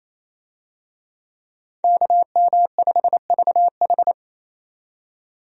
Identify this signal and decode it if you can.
Morse code, heard as a keyed tone, about 28 wpm, 700 Hz. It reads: KM5VH
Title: USA Call Signs - 5 Characters - 28wpm